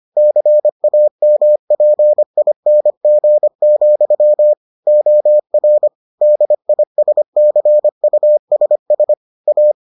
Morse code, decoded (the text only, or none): CAMPING, OR DISCUSS A